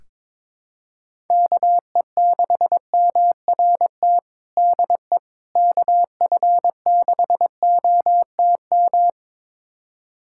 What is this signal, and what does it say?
Morse code, heard as a keyed tone, about 23 wpm, 700 Hz. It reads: KE6MRT DE KF6OTM
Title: USA Callsign DE Callsign - 6 Characters - 22wpm